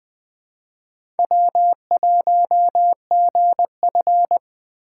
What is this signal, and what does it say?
Morse code, heard as a keyed tone, about 21 wpm, 700 Hz. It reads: W1GF
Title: USA Call Signs - 3x Character Spacing - 20wpm